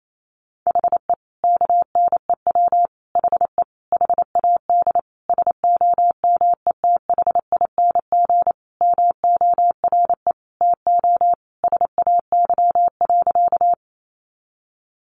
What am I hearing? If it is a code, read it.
Morse code, heard as a keyed tone, about 28 wpm, 700 Hz. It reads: HE KNEW HE HAD SOMETHING MORE TO SAY.
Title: Sentences from Top 600 Words 28wpm